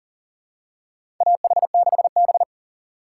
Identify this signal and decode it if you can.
Morse code, heard as a keyed tone, about 40 wpm, 700 Hz. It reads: AH6B
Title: USA Call Signs - 4 Characters - 3x Character Spacing - 40wpm